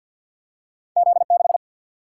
Morse code, heard as a keyed tone, about 50 wpm, 700 Hz. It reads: Z 6
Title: Single-Letter Single-Number 50wpm